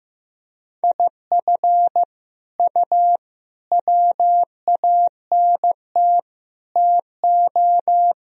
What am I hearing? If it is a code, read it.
Morse code, heard as a keyed tone, about 16 wpm, 700 Hz. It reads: IF U WANT TO
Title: Sentences from Top 200 Words and Abbreviations 15wpm